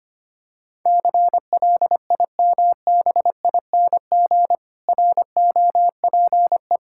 Morse code, heard as a keyed tone, about 25 wpm, 700 Hz. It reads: CLIMBING ROPE
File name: practice / 25wpm / Difficult Words with Hints - 25wpm